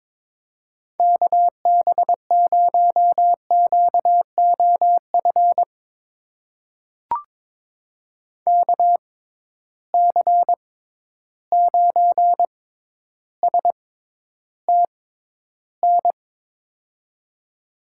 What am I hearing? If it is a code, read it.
Morse code, heard as a keyed tone, about 22 wpm, spaced as for 11 wpm, 700 Hz. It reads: KB0QOF KC9STN
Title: USA Call Signs - 6x Character Spacing - 22wpm